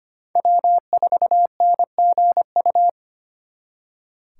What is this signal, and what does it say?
Morse code, heard as a keyed tone, about 32 wpm, 700 Hz. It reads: W4NGU